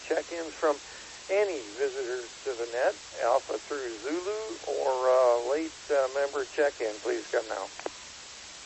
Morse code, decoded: none

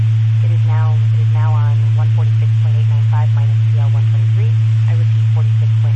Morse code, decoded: none

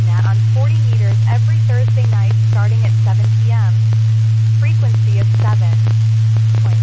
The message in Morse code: none

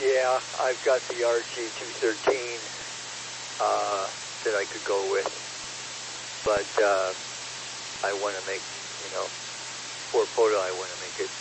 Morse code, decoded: none

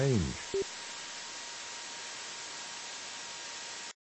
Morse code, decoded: none

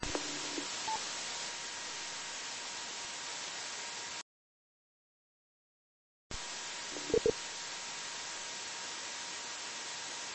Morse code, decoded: none

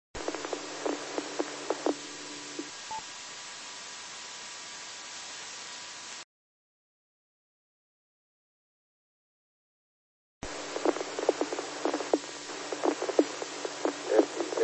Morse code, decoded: none

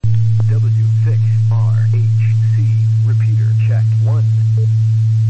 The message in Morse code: none

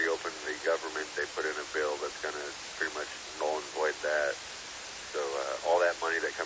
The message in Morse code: none

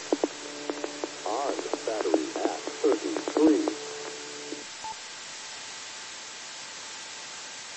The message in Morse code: none

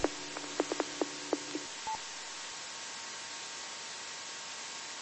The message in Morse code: none

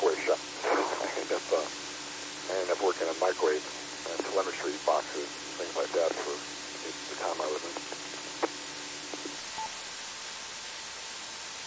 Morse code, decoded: none

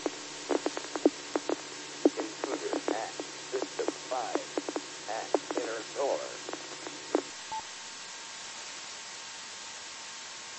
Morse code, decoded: none

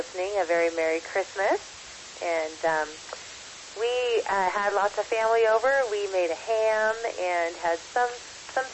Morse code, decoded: none

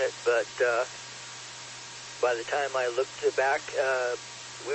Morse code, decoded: none

